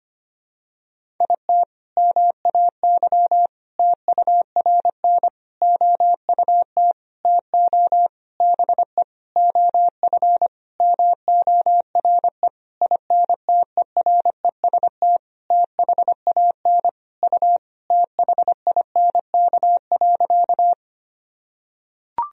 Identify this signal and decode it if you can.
Morse code, heard as a keyed tone, about 25 wpm, 700 Hz. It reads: IT MAY TURN OUT TO BE OF MORE INTEREST THAN U THINK.
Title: Sentences from Top 400 Words and Abbreviations 25wpm